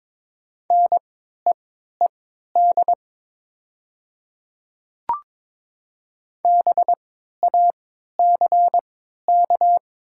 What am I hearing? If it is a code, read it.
Morse code, heard as a keyed tone, about 22 wpm, spaced as for 13 wpm, 700 Hz. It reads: NEED BACK
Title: 3x Character Spacing - Top 300 Words - 22wpm